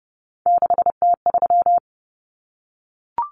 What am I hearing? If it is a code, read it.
Morse code, heard as a keyed tone, about 30 wpm, 700 Hz. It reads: 6T3
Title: Common Cut Numbers - 3-Digit 30wpm